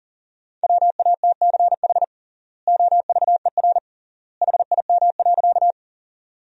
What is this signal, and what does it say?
Morse code, heard as a keyed tone, about 40 wpm, 700 Hz. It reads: WATCH OVER HIM.
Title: 3x Word-Spacing - Sentences from Top 300 Words - 40wpm